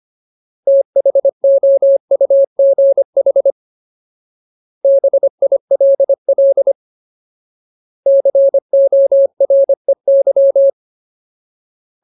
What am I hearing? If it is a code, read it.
Morse code, heard as a keyed tone, about 25 wpm, 550 Hz. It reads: THOUGH BILL COREY
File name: Ultimate Word List / 25WPM Ultimate